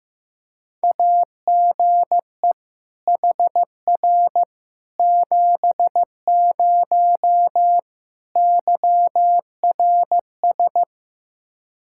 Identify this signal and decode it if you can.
Morse code, heard as a keyed tone, about 15 wpm, 700 Hz. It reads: AGE HR 70 YRS
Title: Practice QSO - Exchange 3 - More Info - 15wpm